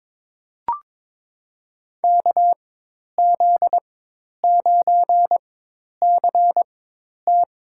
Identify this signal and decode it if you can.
Morse code, heard as a keyed tone, about 21 wpm, spaced as for 10 wpm, 700 Hz. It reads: KZ9CT
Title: USA Call Signs - 4x Character Spacing - 22wpm